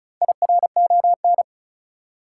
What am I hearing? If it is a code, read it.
Morse code, heard as a keyed tone, about 36 wpm, 700 Hz. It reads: IRON
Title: Four Letter Words - 35wpm